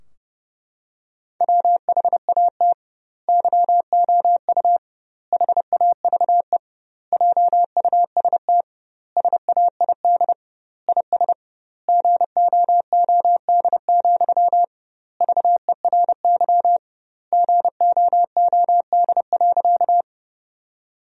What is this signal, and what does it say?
Morse code, heard as a keyed tone, about 30 wpm, 700 Hz. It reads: WHAT YOU HAVE JUST SAID IS GOOD, VERY GOOD.
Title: 2x Word-Spacing - Sentences from Top 200 Words - 30wpm